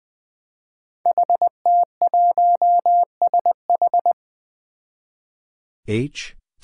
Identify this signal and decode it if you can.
Morse code, heard as a keyed tone, about 20 wpm, 700 Hz. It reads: HT1SH
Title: Five Challenging Letter-Number Combinations 20wpm